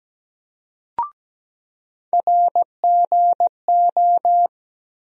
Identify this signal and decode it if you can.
Morse code, heard as a keyed tone, about 17 wpm, 700 Hz. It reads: RGO
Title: USA Call Sign Postfixes - 3 Letters 17wpm